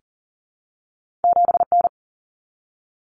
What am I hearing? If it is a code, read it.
Morse code, heard as a keyed tone, about 38 wpm, 700 Hz. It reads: 7N
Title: Common Cut Numbers - 2-Digit 40wpm